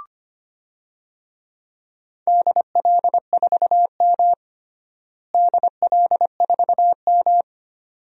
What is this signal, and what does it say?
Morse code, heard as a keyed tone, about 25 wpm, 700 Hz. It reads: DL4M DL4M
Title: DXCC-100 Call Signs - 4 Characters - Repeated 2x - 25wpm